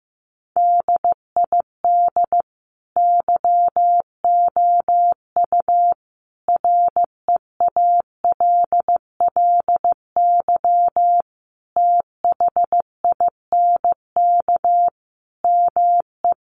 Morse code, heard as a keyed tone, about 15 wpm, 700 Hz. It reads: DID YOU REALLY THINK ME
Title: Sentences from Top 1200 Words 15wpm